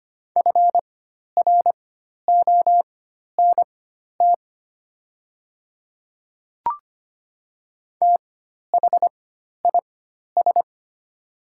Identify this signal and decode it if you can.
Morse code, heard as a keyed tone, about 25 wpm, spaced as for 12 wpm, 700 Hz. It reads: FRONT THIS
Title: 4x Character Spacing - Top 500 Words - 25wpm